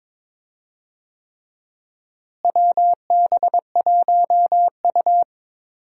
Morse code, heard as a keyed tone, about 22 wpm, 700 Hz. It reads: WB1U
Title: USA Call Signs - 4 Characters - Repeated 3x - 22wpm